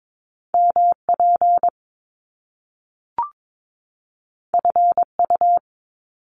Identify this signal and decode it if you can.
Morse code, heard as a keyed tone, about 22 wpm, 700 Hz. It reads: MP FU
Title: Top Two Letter Combinations in Words 22wpm